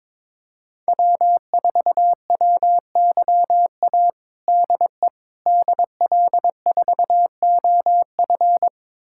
Morse code, heard as a keyed tone, about 22 wpm, 700 Hz. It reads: W4WYA DE DL4OF
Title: USA Callsign DE Callsign - 5 Characters - 22wpm